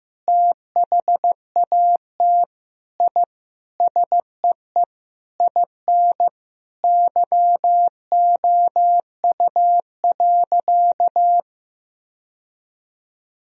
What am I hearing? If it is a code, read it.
Morse code, heard as a keyed tone, about 15 wpm, 700 Hz. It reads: THAT I SEE IN YOU.
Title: Sentences from Top 1000 Words - Review Part 2 - 15wpm